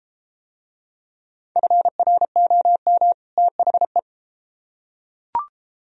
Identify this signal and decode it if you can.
Morse code, heard as a keyed tone, about 33 wpm, 700 Hz. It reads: FROM THE